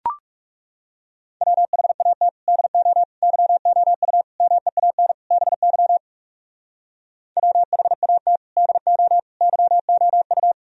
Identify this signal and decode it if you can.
Morse code, heard as a keyed tone, about 37 wpm, 700 Hz. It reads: WHAT DO YOU MEAN BY WHAT DO YOU
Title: Speed-Racing - Sets of 5 Words 30wpm